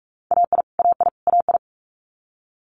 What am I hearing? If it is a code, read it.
Morse code, heard as a keyed tone, about 46 wpm, 700 Hz. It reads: AI AI AI
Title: Multi-letter Phonemes Repeated 3x 45wpm